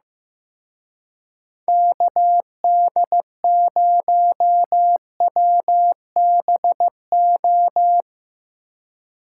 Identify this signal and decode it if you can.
Morse code, heard as a keyed tone, about 17 wpm, 700 Hz. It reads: KD0WBO
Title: Speed Racing - USA Call Signs 15wpm